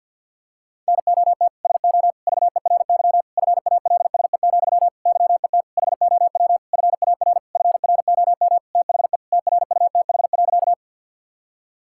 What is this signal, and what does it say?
Morse code, heard as a keyed tone, about 50 wpm, 700 Hz. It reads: NOT SO VERY FALSE, YET HOW FAR FROM THE TRUTH.
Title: Sentences from Top 1600 Words 50wpm